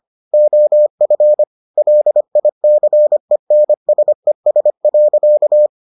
Morse code, heard as a keyed tone, about 25 wpm, 600 Hz. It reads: OF LICENSES.